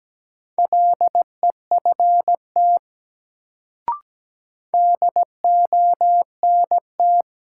Morse code, heard as a keyed tone, about 17 wpm, 700 Hz. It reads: LEFT DONT
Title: Sentences from Top 300 Words - New Word Drill - 17wpm